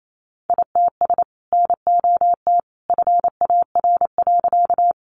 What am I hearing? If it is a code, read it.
Morse code, heard as a keyed tone, about 28 wpm, 700 Hz. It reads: ITS NOT FAR.
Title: Sentences from Top 600 Words 28wpm